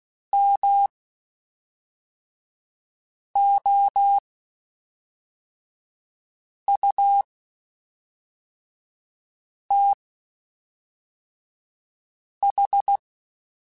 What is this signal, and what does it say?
Morse code, heard as a keyed tone, about 16 wpm, spaced as for 3 wpm, 800 Hz. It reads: MOUTH
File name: Quotes Spaced / Quotes Spaced-mp3s / HoofInMouth sp @5